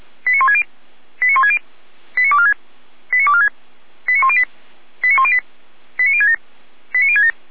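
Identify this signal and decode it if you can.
Five-tone selective call (ZVEI), heard as quick runs of five tones, tones of about 70 ms. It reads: 89160 89160 89246 89246 89108 89108 89067 89067
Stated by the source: ZVEI five-tone selective call, received by SDR.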